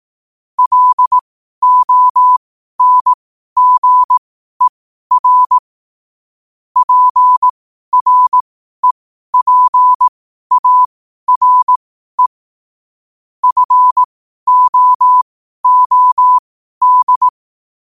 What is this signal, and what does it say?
Morse code, heard as a keyed tone, about 18 wpm, spaced as for 12 wpm, 1000 Hz. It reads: LONGER PREPARE FOOD
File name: qotdfiles / 12wpm